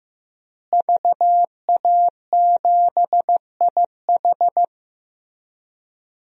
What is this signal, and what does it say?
Morse code, heard as a keyed tone, about 15 wpm, 700 Hz. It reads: VA7IH
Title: DXCC-100 Call Signs - 5 Characters - Repeated 2x - 15wpm